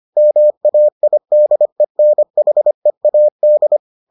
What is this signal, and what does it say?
Morse code, heard as a keyed tone, about 25 wpm, 600 Hz. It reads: MAIDENHEAD